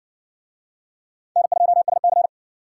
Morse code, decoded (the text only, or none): N2SC